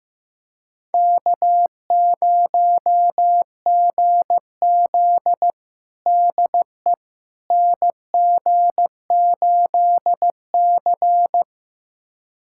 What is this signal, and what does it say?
Morse code, heard as a keyed tone, about 15 wpm, 700 Hz. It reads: K0GZ DE NG8C
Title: USA Callsign DE Callsign - 4 Characters - 15wpm